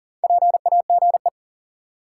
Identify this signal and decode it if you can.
Morse code, heard as a keyed tone, about 40 wpm, 700 Hz. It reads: PAGE
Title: Sentences from Top 1000 Words - Review Part 1 - New Word Drill - 40wpm